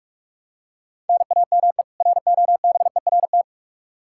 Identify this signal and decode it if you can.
Morse code, heard as a keyed tone, about 46 wpm, 700 Hz. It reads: NAME ROBERT
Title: Practice QSO - Exchange 1 - Beginning - 45wpm